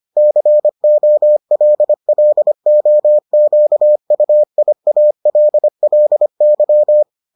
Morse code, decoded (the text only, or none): COLLOQUIALLY